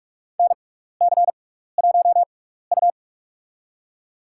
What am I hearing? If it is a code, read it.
Morse code, heard as a keyed tone, about 44 wpm, spaced as for 15 wpm, 700 Hz. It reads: NC1U
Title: USA Call Signs - 4 Characters - 6x Character Spacing - 45wpm